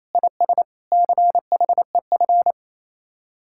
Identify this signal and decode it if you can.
Morse code, heard as a keyed tone, about 37 wpm, 700 Hz. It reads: IS CHEF